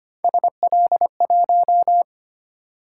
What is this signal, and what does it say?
Morse code, heard as a keyed tone, about 25 wpm, 700 Hz. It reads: SL1
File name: Three Letter-Number Combinations - 25wpm